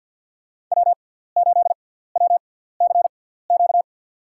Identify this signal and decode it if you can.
Morse code, heard as a keyed tone, about 53 wpm, spaced as for 17 wpm, 700 Hz. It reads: W8WCQ
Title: USA Call Signs - 6x Character Spacing - 50wpm